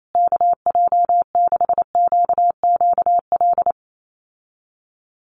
Morse code, heard as a keyed tone, about 28 wpm, 700 Hz. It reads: KJ6QQL
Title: USA Call Signs Repeated 3x 28wpm